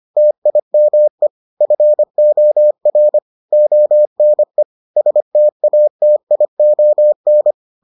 Morse code, heard as a keyed tone, about 25 wpm, 600 Hz. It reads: TIME FOR ONE STATION